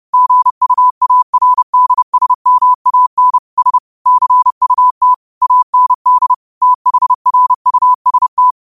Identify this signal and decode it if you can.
Morse code, heard as a keyed tone, about 30 wpm, 1000 Hz. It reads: GUARDSMAN S CUT AND THRUST
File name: qotdfiles / 30wpm